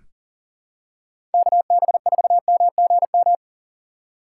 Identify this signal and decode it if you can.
Morse code, heard as a keyed tone, about 39 wpm, 700 Hz. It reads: KB4MGM
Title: USA Call Signs - 6 Characters - 6x Character Spacing - 40wpm